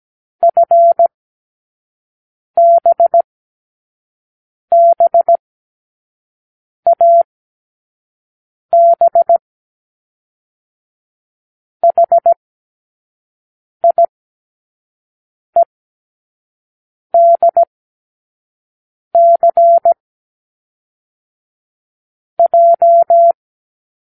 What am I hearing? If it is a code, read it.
Morse code, heard as a keyed tone, about 17 wpm, 680 Hz. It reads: F B B A B H I E D C J
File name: k7qo / Lesson15(A-J)